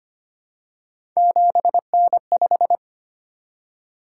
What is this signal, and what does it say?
Morse code, heard as a keyed tone, about 25 wpm, 700 Hz. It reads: 7N5
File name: Three Letter-Number Combinations - 25wpm